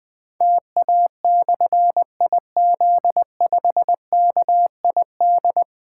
Morse code, heard as a keyed tone, about 20 wpm, 700 Hz. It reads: TA/IZ5KID
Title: Difficult Call Signs 20wpm